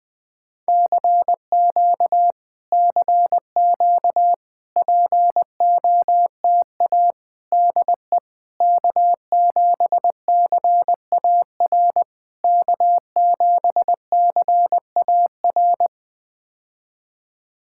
Morse code, heard as a keyed tone, about 20 wpm, 700 Hz. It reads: CQ CQ POTA DE K7CAR K7CAR
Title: POTA QSO Practice - 20wpm